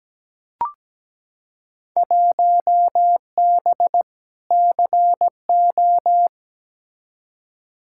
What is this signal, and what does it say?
Morse code, heard as a keyed tone, about 17 wpm, 700 Hz. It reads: 1B CO
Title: Field Day Exchanges - 17wpm